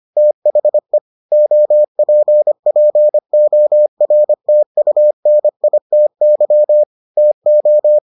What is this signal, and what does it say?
Morse code, heard as a keyed tone, about 25 wpm, 600 Hz. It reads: THE OPPORTUNITY TO